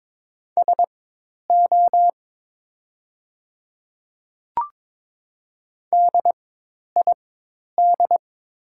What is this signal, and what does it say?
Morse code, heard as a keyed tone, about 22 wpm, spaced as for 10 wpm, 700 Hz. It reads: SO DID
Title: 4x Character Spacing - Top 300 Words - 22wpm